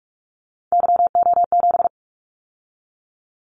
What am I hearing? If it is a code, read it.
Morse code, heard as a keyed tone, about 46 wpm, 700 Hz. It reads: YO7
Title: Callsign Prefixes for Top 25 DXCC - 45wpm